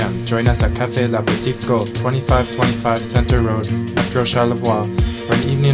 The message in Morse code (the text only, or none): none